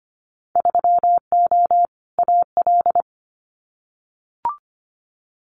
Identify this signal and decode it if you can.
Morse code, heard as a keyed tone, about 25 wpm, 690 Hz. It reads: 3O AL
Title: Winter Field Day Exchanges 25wpm